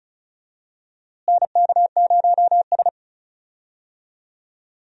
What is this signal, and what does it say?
Morse code, heard as a keyed tone, about 34 wpm, 700 Hz. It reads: NK0S
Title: USA Call Signs - 4 Characters - Repeated 3x - 35wpm